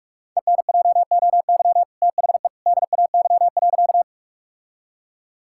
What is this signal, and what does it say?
Morse code, heard as a keyed tone, about 46 wpm, 700 Hz. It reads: ENJOY THE DAY.
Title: Sentences from Top 1000 Words - Review Part 5 - 45wpm